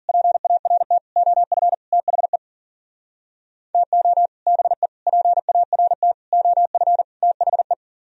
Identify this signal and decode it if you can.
Morse code, heard as a keyed tone, about 51 wpm, 700 Hz. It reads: PART OF THE TO BE PART OF THE